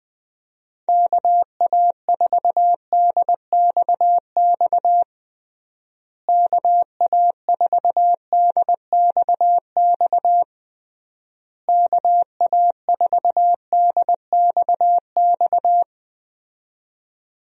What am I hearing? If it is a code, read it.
Morse code, heard as a keyed tone, about 20 wpm, 700 Hz. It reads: KA4DXX KA4DXX KA4DXX
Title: USA Call Signs - 6 Characters - Repeated 3x - 20wpm